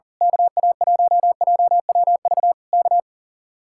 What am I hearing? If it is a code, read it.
Morse code, heard as a keyed tone, about 38 wpm, 700 Hz. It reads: KA1JWV K
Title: Speed-Racing Common QSO Phrases - 40wpm